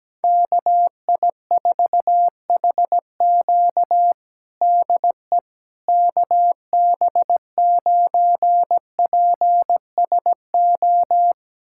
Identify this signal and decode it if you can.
Morse code, heard as a keyed tone, about 17 wpm, 700 Hz. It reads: KI4HQ DE KB9PSO
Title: USA Callsign DE Callsign 17wpm